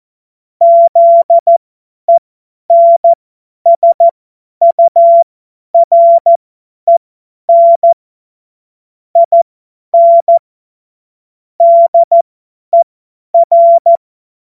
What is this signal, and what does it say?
Morse code, heard as a keyed tone, about 14 wpm, spaced as for 10 wpm, 680 Hz. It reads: ZENSUREN IN DER
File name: Morsekurs / Witze / Witz 10. Januar 2023 50 F70